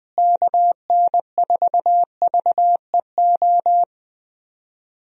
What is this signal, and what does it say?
Morse code, heard as a keyed tone, about 20 wpm, 700 Hz. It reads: KN4VEO
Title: USA Call Signs - 6 Characters - Repeated 3x - 20wpm